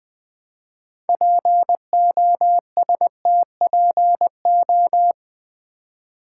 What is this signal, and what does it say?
Morse code, heard as a keyed tone, about 20 wpm, 700 Hz. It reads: POSTPO
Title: Build A Word - 3 Letters At A Time - 8-Letter Words - 20wpm